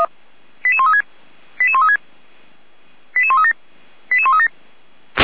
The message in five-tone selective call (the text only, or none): 88136 88136 88137 88137